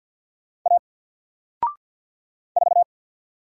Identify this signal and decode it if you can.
Morse code, heard as a keyed tone, about 48 wpm, 700 Hz. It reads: A 4
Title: Single Letter-Number Alt 50wpm